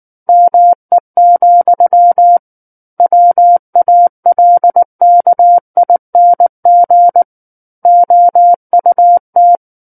Morse code, heard as a keyed tone, about 19 wpm, 700 Hz. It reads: ME, WALKING OUT
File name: Book One / 10-Chapter 10 - 19 WPM